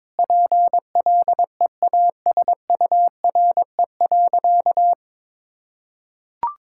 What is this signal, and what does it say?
Morse code, heard as a keyed tone, about 22 wpm, 700 Hz. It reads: PLEASURE.
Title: Sentences from Top 1600 Words 22wpm